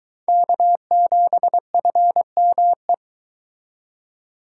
Morse code, heard as a keyed tone, about 22 wpm, 700 Hz. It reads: K7FME